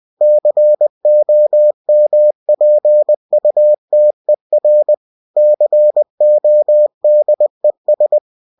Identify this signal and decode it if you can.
Morse code, heard as a keyed tone, about 20 wpm, 600 Hz. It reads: COMPUTER CODES